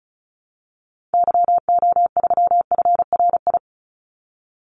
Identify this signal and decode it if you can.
Morse code, heard as a keyed tone, about 34 wpm, 700 Hz. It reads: YO3FRI
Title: Call Signs Repeated 3x - 35wpm